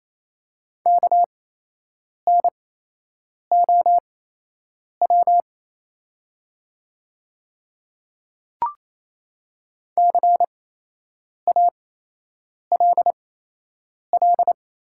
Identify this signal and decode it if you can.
Morse code, heard as a keyed tone, about 28 wpm, spaced as for 8 wpm, 700 Hz. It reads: KNOW CALL
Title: 8x Character Spacing - Top 100 Words - 28wpm